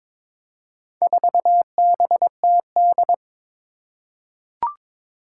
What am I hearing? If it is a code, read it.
Morse code, heard as a keyed tone, about 22 wpm, 700 Hz. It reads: 4BTD
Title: Four Challenging Letter-Number Combinations 22wpm